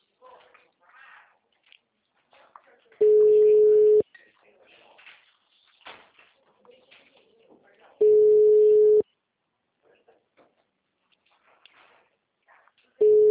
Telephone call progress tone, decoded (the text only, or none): ringback tone